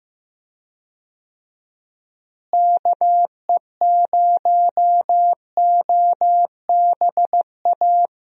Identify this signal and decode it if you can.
Morse code, heard as a keyed tone, about 15 wpm, 700 Hz. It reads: KE0OBA
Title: USA Call Signs - 6 Characters - Repeated 2x - 15wpm